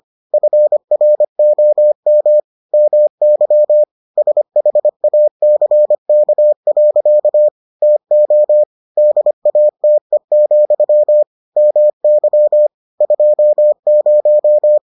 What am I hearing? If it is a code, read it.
Morse code, heard as a keyed tone, about 25 wpm, 600 Hz. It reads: FROM MY SHACK. TO DATE, MY 20